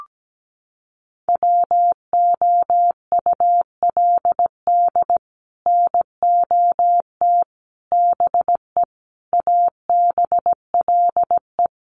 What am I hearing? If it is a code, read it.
Morse code, heard as a keyed tone, about 17 wpm, 700 Hz. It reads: WOULD NOT BE ABLE
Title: Sets of 5 Words 17wpm